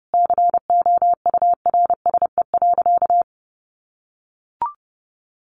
Morse code, heard as a keyed tone, about 30 wpm, 700 Hz. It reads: COURSE.
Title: Sentences from Top 1500 Words 30wpm